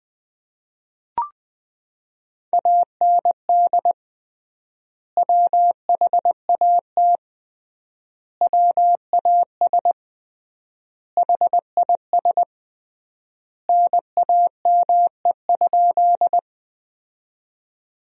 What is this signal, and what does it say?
Morse code, heard as a keyed tone, about 20 wpm, 700 Hz. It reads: AND WHAT WAS HIS NAME?
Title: 3x Word-Spacing - Sentences from Top 200 Words - 20wpm